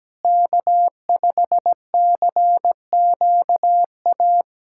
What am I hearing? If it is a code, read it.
Morse code, heard as a keyed tone, about 17 wpm, 700 Hz. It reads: K5CQA
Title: USA Callsign DE Callsign - 5 Characters - 17wpm